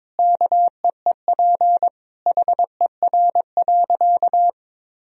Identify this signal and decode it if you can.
Morse code, heard as a keyed tone, about 22 wpm, 700 Hz. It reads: KEEP HER.
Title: Sentences from Top 1000 Words - Review Part 2 - 22wpm